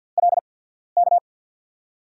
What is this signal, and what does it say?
Morse code, heard as a keyed tone, about 48 wpm, spaced as for 13 wpm, 700 Hz. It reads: LK
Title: USA Call Signs - 5 Characters - 8x Character Spacing - 50wpm